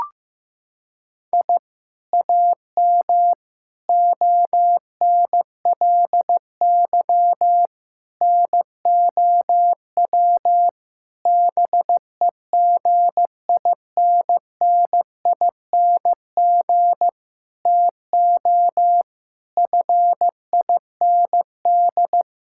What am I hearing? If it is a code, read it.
Morse code, heard as a keyed tone, about 15 wpm, 700 Hz. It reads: I AM ONLY NOW BEGINNING TO FIND
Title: Sentences from Top 1000 Words - Review Part 5 - 15wpm